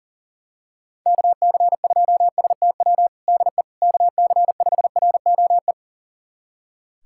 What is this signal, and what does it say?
Morse code, heard as a keyed tone, about 40 wpm, 700 Hz. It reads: KC2STW DE KC5ROE
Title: USA Callsign DE Callsign - 6 Characters - 40wpm